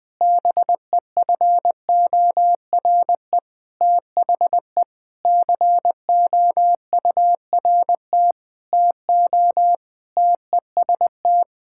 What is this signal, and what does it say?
Morse code, heard as a keyed tone, about 21 wpm, 700 Hz. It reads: BEFORE THE COURT TO TEST
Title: Sentences from Top 1600 Words 20wpm